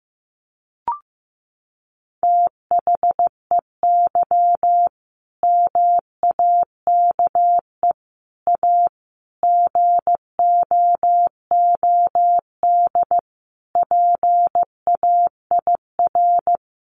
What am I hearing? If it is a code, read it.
Morse code, heard as a keyed tone, about 15 wpm, 700 Hz. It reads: THEY MAKE A GOOD PAIR
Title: Sentences from Top 1200 Words 15wpm